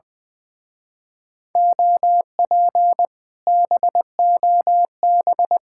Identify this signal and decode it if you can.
Morse code, heard as a keyed tone, about 20 wpm, 700 Hz. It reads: OP BOB
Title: Speed-Racing Common QSO Phrases - 20wpm